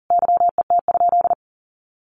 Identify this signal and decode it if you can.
Morse code, heard as a keyed tone, about 40 wpm, 700 Hz. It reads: YET?
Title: Sentences from Top 1300 Words 40wpm